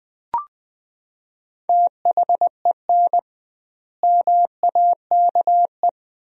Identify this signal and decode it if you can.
Morse code, heard as a keyed tone, about 20 wpm, 700 Hz. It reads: THEN MAKE
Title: 2x Word-Spacing - Sentences from Top 100 Words - 20wpm